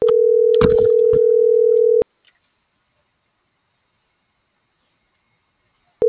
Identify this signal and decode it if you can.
Ringback tone (North America), 440 + 480 Hz.